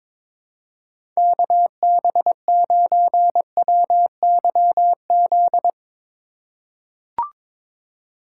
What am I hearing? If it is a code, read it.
Morse code, heard as a keyed tone, about 22 wpm, 700 Hz. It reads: KB9WYZ
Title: USA Call Signs - 2x Character Spacing - 22wpm